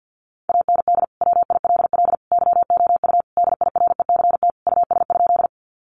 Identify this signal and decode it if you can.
Morse code, heard as a keyed tone, about 50 wpm, 700 Hz. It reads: AND WILL YOU DIRECT US?
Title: Sentences from Top 400 Words 50wpm